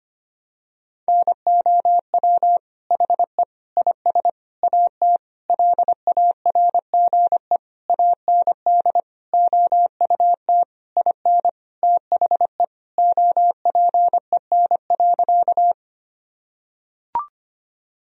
Morse code, heard as a keyed tone, about 25 wpm, 700 Hz. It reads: NOW HE IS AT LARGE AND OUT IN THE OPEN.
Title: Sentences from Top 1000 Words - Review Part 1 - 25wpm